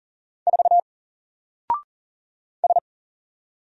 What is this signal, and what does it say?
Morse code, heard as a keyed tone, about 40 wpm, 700 Hz. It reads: 4 S
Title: Single Letter-Number 40wpm